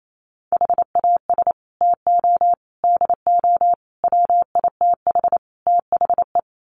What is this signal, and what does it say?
Morse code, heard as a keyed tone, about 28 wpm, 700 Hz. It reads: HAS TO DO WITH THE
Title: Sets of 5 Words 28wpm